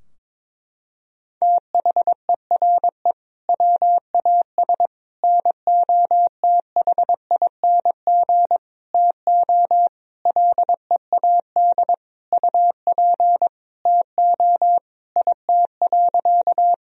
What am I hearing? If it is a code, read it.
Morse code, heard as a keyed tone, about 22 wpm, 700 Hz. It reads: THERE WAS NOTHING TO LEAD UP TO IT.
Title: Sentences from Top 1000 Words - Review Part 2 - 22wpm